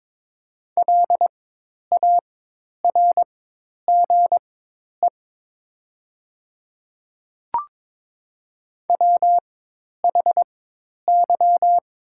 Morse code, heard as a keyed tone, about 23 wpm, spaced as for 10 wpm, 700 Hz. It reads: LARGE WHY
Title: 4x Character Spacing - Top 300 Words - 22wpm